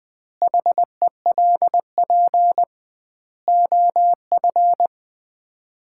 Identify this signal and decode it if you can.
Morse code, heard as a keyed tone, about 21 wpm, 700 Hz. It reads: HELP OF
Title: 2x Word-Spacing - Sets of 4 Words - 20wpm